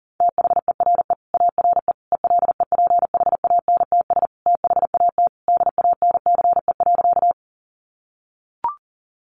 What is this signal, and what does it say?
Morse code, heard as a keyed tone, about 40 wpm, 700 Hz. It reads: THERE ARE ELEPHANTS THAT DANCE.
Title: Easy Phrases 40wpm